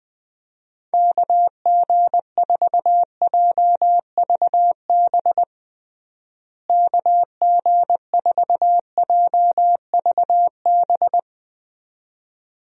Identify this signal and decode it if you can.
Morse code, heard as a keyed tone, about 20 wpm, 700 Hz. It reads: KG4JVB KG4JVB
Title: USA Call Signs - 6 Characters - Repeated 2x - 20wpm